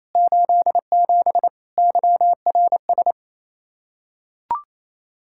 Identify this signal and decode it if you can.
Morse code, heard as a keyed tone, about 28 wpm, 700 Hz. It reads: 87 YRS
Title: Practice QSO - Exchange 3 - More Info - 28wpm